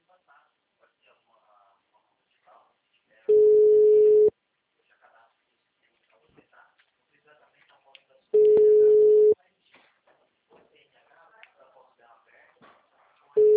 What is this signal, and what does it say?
Ringback tone (Europe, 425 Hz), with speech.